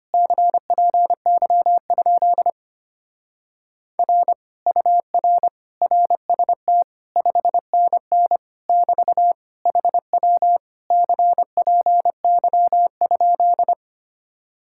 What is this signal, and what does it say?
Morse code, heard as a keyed tone, about 29 wpm, 700 Hz. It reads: CPY? R UR RST 5NN <BT> HW CPY?
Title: Speed-Racing Common QSO Phrases Encore - 25wpm